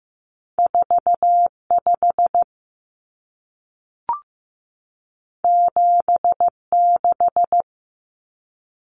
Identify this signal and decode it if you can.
Morse code, heard as a keyed tone, about 15 wpm, 700 Hz. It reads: 4 5 7 6
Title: Numbers - 2-Digit 15wpm